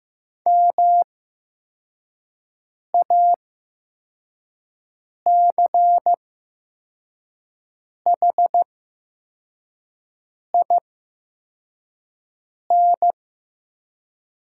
Morse code, heard as a keyed tone, about 15 wpm, spaced as for 4 wpm, 700 Hz. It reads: MACHIN
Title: 8x Character Spacing - Top 500 Words - 15wpm